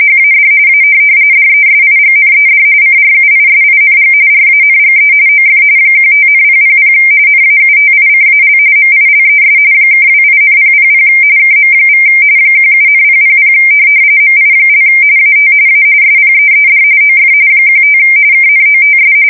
RTTY, about 45.45 baud. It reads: HAD A DESIGNED CRUISING RANGE OF 15,000 NAUTICAL MILES (28,000 KM= 17,000 MI) AT A SPEED OF 15 KNOTS
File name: fsk170 / USS-Missouri